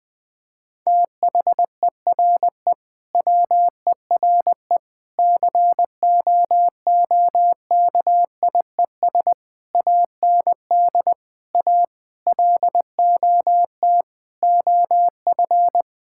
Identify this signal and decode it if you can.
Morse code, heard as a keyed tone, about 20 wpm, 700 Hz. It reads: THERE WERE COOKIES AND A LOT OF
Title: Sentences with 4-Word Phrases - 20wpm